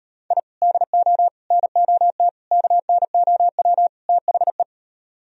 Morse code, heard as a keyed tone, about 36 wpm, 700 Hz. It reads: I DO NOT KNOW THE